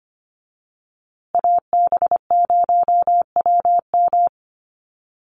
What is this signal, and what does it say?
Morse code, heard as a keyed tone, about 25 wpm, 700 Hz. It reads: AB0WM
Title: USA Call Signs Repeated 2x 25wpm